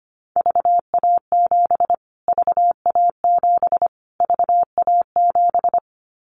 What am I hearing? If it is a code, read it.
Morse code, heard as a keyed tone, about 25 wpm, 700 Hz. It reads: VA7 VA7 VA7
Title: Callsign Prefixes for Top 25 DXCC - Repeated 3x - 25wpm